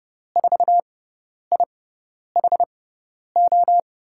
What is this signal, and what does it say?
Morse code, heard as a keyed tone, about 30 wpm, spaced as for 10 wpm, 700 Hz. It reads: 4IHO
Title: USA Call Signs - 6x Character Spacing - 30wpm